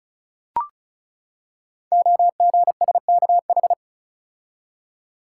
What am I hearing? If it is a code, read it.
Morse code, heard as a keyed tone, about 36 wpm, 700 Hz. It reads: OGSKH
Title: Code Groups - 5 Letters - 35wpm